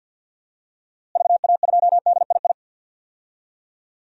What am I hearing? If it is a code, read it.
Morse code, heard as a keyed tone, about 52 wpm, 700 Hz. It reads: VA2DII
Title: DXCC-100 Call Signs - 6 Characters - Repeated 2x - 50wpm